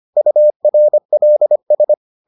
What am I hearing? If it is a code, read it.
Morse code, heard as a keyed tone, about 25 wpm, 600 Hz. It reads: URLS